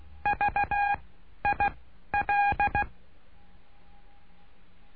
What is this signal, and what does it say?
Morse code, heard as a keyed tone, about 16 wpm, 820 Hz. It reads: V I L
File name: cursos / CURSO DE TELEGRAFIA / Telegrafia - Trilha 09